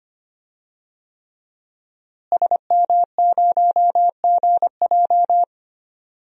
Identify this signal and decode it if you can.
Morse code, heard as a keyed tone, about 25 wpm, 700 Hz. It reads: SM0GJ
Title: DXCC-100 Call Signs - 5 Characters - Repeated 3x - 25wpm